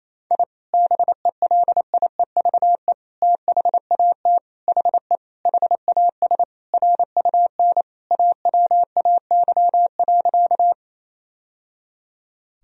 Morse code, heard as a keyed tone, about 28 wpm, 700 Hz. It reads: I BELIEVE THAT HE HAS RUN AWAY.
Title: Sentences from Top 1000 Words - Review Part 1 - 28wpm